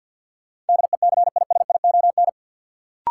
Morse code, heard as a keyed tone, about 52 wpm, 700 Hz. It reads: DECISION
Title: Sentences from Top 1000 Words - New Word Drill - 50wpm